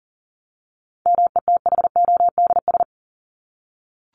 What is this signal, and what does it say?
Morse code, heard as a keyed tone, about 40 wpm, 700 Hz. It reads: METHODS
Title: ES:S Suffix - Top 100 Words with Base Words 40wpm